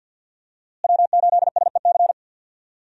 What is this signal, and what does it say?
Morse code, heard as a keyed tone, about 52 wpm, 700 Hz. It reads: W8SEC
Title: USA Call Signs - 5 Characters - Repeated 2x - 50wpm